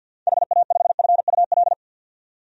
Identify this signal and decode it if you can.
Morse code, heard as a keyed tone, about 51 wpm, 700 Hz. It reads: HA5FUL
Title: DXCC-100 Call Signs - 6 Characters - 50wpm